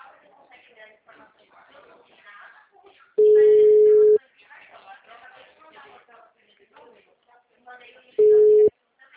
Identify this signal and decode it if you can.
Ringback tone (Europe, 425 Hz), with speech.